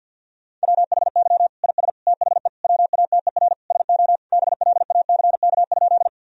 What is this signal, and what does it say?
Morse code, heard as a keyed tone, about 50 wpm, 700 Hz. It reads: WHY IS THE WATER SO BLACK?
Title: Sentences from Top 1000 Words - Review Part 2 - 50wpm